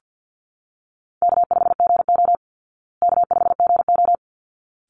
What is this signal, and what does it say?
Morse code, heard as a keyed tone, about 50 wpm, 700 Hz. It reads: K5GO K5GO
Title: Call Signs Repeated 3x - 50wpm